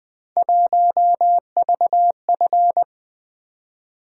Morse code, heard as a keyed tone, about 20 wpm, 700 Hz. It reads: 1VF
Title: Three Letter-Number Combinations - 20wpm